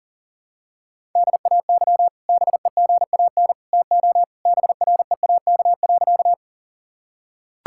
Morse code, heard as a keyed tone, about 40 wpm, 700 Hz. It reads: DAY BEGAN TO BREAK.
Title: Sentences from Top 1000 Words - Review Part 6 - 40wpm